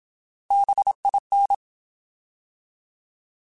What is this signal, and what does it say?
Morse code, heard as a keyed tone, about 27 wpm, 800 Hz. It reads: BIN
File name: Alphabetical / Alphabetical mp3s / Bs @25